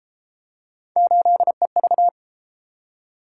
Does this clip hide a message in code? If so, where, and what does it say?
Yes: Morse code from 1.0 to 2.1 s, heard as a keyed tone, about 28 wpm, 700 Hz. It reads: 8EV